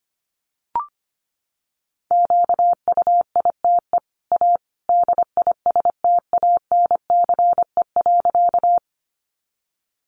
Morse code, heard as a keyed tone, about 25 wpm, 700 Hz. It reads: QUITE A DISTANCE.
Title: Sentences from Top 1200 Words 25wpm